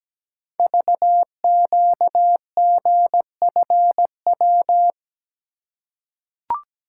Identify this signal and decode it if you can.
Morse code, heard as a keyed tone, about 17 wpm, 700 Hz. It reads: VQGFW
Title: Code Groups - 5 Letters - 17wpm